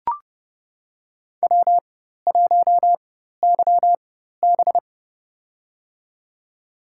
Morse code, heard as a keyed tone, about 30 wpm, spaced as for 14 wpm, 700 Hz. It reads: W1YB
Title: USA Call Signs - 4 Characters - 4x Character Spacing - 30wpm